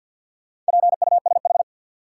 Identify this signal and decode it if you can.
Morse code, heard as a keyed tone, about 53 wpm, 700 Hz. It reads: PUSH
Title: Sentences from Top 1300 Words - New Word Drill Alt - 50wpm